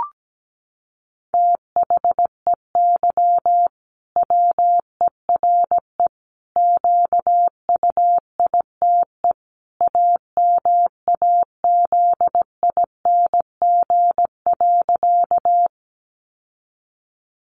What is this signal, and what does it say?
Morse code, heard as a keyed tone, about 17 wpm, 700 Hz. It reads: THEY WERE QUITE AMAZING.
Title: Sentences from Top 1500 Words 17wpm